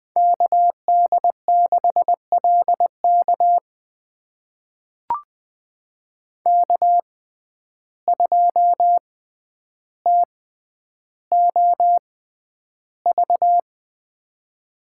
Morse code, heard as a keyed tone, about 20 wpm, spaced as for 10 wpm, 700 Hz. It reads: KD6LK K2TOV
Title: USA Call Signs - 5 Characters - 6x Character Spacing - 20wpm